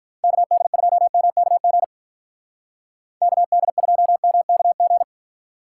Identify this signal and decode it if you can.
Morse code, heard as a keyed tone, about 44 wpm, 700 Hz. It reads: KD2MKG KD2MKG
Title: Speed Racing - USA 2x3 Call Signs 35wpm